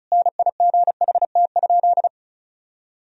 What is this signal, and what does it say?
Morse code, heard as a keyed tone, about 34 wpm, 700 Hz. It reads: NIGHT?
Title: Sentences from Top 1000 Words - Review Part 2 - 35wpm